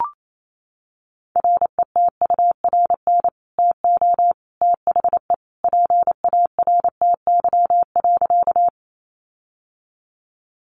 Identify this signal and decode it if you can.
Morse code, heard as a keyed tone, about 28 wpm, 700 Hz. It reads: RETURN TO THE PARTY.
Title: Sentences from Top 1400 Words 28wpm